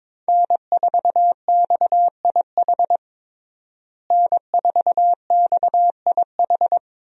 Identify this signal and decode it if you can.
Morse code, heard as a keyed tone, about 21 wpm, 700 Hz. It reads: N4XIH N4XIH
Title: USA Call Signs - 5 Characters - Repeated 2x - 22wpm